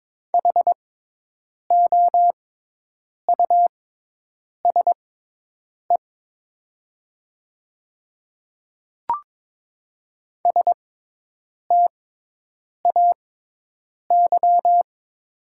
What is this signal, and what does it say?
Morse code, heard as a keyed tone, about 22 wpm, spaced as for 8 wpm, 700 Hz. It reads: HOUSE STAY
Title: 6x Character Spacing - Top 500 Words - 22wpm